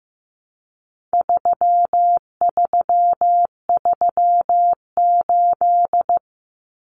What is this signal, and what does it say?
Morse code, heard as a keyed tone, about 15 wpm, 700 Hz. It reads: 3338
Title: Common Cut Numbers - 4-Digit 15wpm